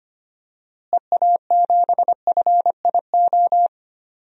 Morse code, heard as a keyed tone, about 25 wpm, 700 Hz. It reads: EA7FIO
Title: DXCC-100 Call Signs - 6 Characters - Repeated 2x - 25wpm